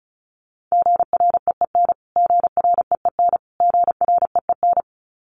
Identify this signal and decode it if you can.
Morse code, heard as a keyed tone, about 34 wpm, 700 Hz. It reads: GREEN GREEN GREEN
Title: Top 500 Words Repeated 35wpm